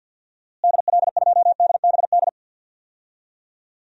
Morse code, heard as a keyed tone, about 52 wpm, 700 Hz. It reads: DL2DBD
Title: DXCC-100 Call Signs - 6 Characters - Repeated 2x - 50wpm